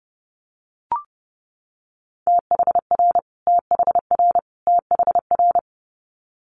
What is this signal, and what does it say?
Morse code, heard as a keyed tone, about 30 wpm, 700 Hz. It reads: THR THR THR
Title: Multi-letter Phonemes Repeated 3x 30wpm